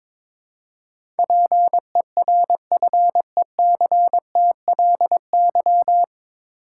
Morse code, heard as a keyed tone, about 22 wpm, 700 Hz. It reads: PERFECTLY
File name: 9 Letter Words - First 3 Letters Given - 22wpm